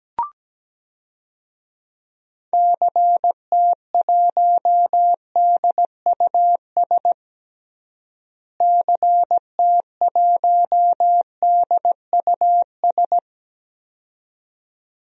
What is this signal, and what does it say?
Morse code, heard as a keyed tone, about 17 wpm, 700 Hz. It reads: CT1DUS CT1DUS
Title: DXCC-100 Call Signs - 6 Characters - Repeated 2x - 17wpm